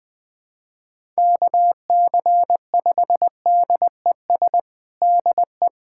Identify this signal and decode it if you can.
Morse code, heard as a keyed tone, about 21 wpm, 700 Hz. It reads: KC5DES DE
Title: USA Callsign DE Callsign - 6 Characters - 20wpm